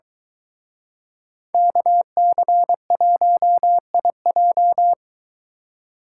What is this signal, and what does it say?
Morse code, heard as a keyed tone, about 22 wpm, 700 Hz. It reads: KC1IJ